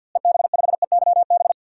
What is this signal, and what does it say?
Morse code, heard as a keyed tone, about 52 wpm, 700 Hz. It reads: EB5EYB
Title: DXCC-100 Call Signs - 6 Characters - Repeated 2x - 50wpm